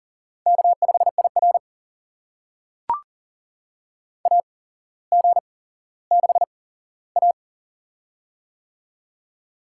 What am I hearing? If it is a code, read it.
Morse code, heard as a keyed tone, about 40 wpm, spaced as for 17 wpm, 700 Hz. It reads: K5IR AG6A
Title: USA Call Signs - 8x Character Spacing - 40wpm